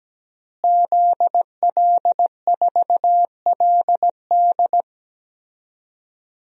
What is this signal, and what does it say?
Morse code, heard as a keyed tone, about 17 wpm, 700 Hz. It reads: ZL4LD
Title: DXCC-100 Call Signs - 5 Characters - Repeated 3x - 17wpm